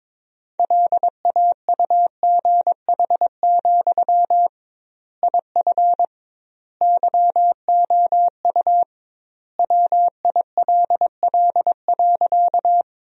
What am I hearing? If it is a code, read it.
Morse code, heard as a keyed tone, about 22 wpm, 700 Hz. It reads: LAUGH, IF YOU WILL.
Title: 2x Word-Spacing - Sentences from Top 500 Words - 22wpm